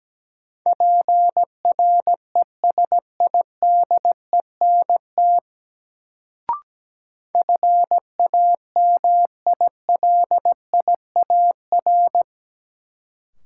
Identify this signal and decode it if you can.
Morse code, heard as a keyed tone, about 17 wpm, 700 Hz. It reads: PRESIDENT FAMILIAR
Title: Sentences from Top 1000 Words - New Word Drill - 17wpm